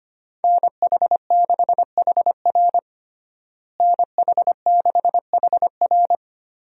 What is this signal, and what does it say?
Morse code, heard as a keyed tone, about 25 wpm, 700 Hz. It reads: NH6HR NH6HR
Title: USA Call Signs - 5 Characters - Repeated 3x - 25wpm